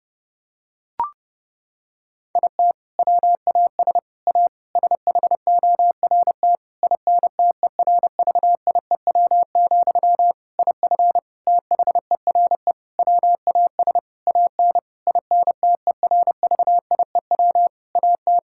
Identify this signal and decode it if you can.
Morse code, heard as a keyed tone, about 30 wpm, 700 Hz. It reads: IT WAS A SHORT INTERVIEW, IF THERE WAS AN INTERVIEW AT
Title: Sentences from Top 1600 Words 30wpm